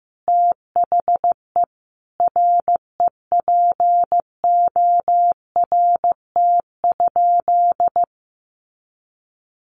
Morse code, heard as a keyed tone, about 15 wpm, 700 Hz. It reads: THE REPORT?
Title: Sentences from Top 900 Words 15wpm